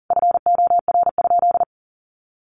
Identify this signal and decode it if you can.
Morse code, heard as a keyed tone, about 40 wpm, 700 Hz. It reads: FOR?
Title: Sentences from Top 1400 Words 40wpm